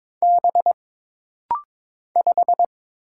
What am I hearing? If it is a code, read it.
Morse code, heard as a keyed tone, about 22 wpm, 700 Hz. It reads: B 5
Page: This rapid-fire timing is geared to push your ICR (Instant Character Recognition) skills.